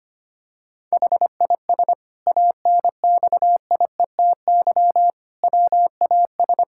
Morse code, heard as a keyed tone, about 25 wpm, 700 Hz. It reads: HIS ANXIETY WAS
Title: Sentences from Top 1600 Words 25wpm